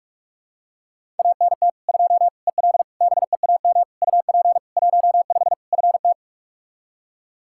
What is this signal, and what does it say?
Morse code, heard as a keyed tone, about 46 wpm, 700 Hz. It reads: ANT 2 EL BEAM UP 15 FT
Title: Practice QSO - Exchange 2 - Info - 45wpm